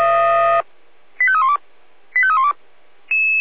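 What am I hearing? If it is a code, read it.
Five-tone selective call (ZVEI), heard as quick runs of five tones, tones of about 70 ms. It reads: 85312 85312
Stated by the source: ZVEI five-tone selective call 85310, received by SDR, on the Feuerwehr Orlamünde audio server.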